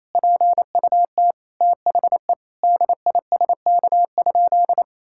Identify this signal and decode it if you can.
Morse code, heard as a keyed tone, about 28 wpm, 690 Hz. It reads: PUT THE DISK?
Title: Sentences from Top 1000 Words - Review Part 6 - 28wpm